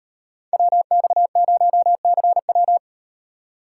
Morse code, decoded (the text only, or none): WX0CW